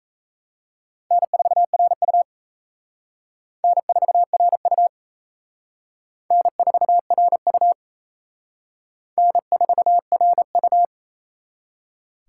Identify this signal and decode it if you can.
Morse code, heard as a keyed tone, about 34 wpm, 700 Hz. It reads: N4RU N4RU N4RU N4RU
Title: Speed Racing - USA 1x2 Call Signs 28wpm